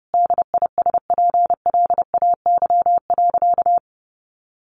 Morse code, heard as a keyed tone, about 30 wpm, 700 Hz. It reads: DISPLAY.
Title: Sentences from Top 1200 Words 30wpm